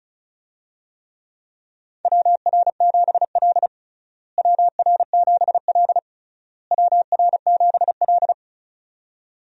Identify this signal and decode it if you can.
Morse code, heard as a keyed tone, about 34 wpm, 700 Hz. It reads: WR7L WR7L WR7L
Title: USA Call Signs - 4 Characters - Repeated 3x - 35wpm